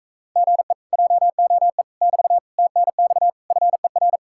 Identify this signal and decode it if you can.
Morse code, heard as a keyed tone, about 37 wpm, 700 Hz. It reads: GE JOE <BT> TNX FER